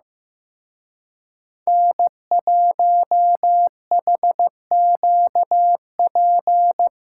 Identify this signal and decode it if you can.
Morse code, heard as a keyed tone, about 18 wpm, 700 Hz. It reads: N1HQP
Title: Speed Racing - USA 1x3 Call Signs 15wpm